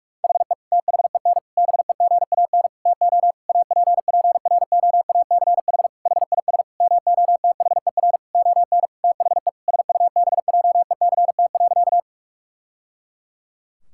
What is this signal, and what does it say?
Morse code, heard as a keyed tone, about 46 wpm, 700 Hz. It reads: HE THEN BEGAN TO APPROACH HIS MOTHER ON THE SUBJECT.
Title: Sentences from Top 1000 Words - Review Part 6 - 45wpm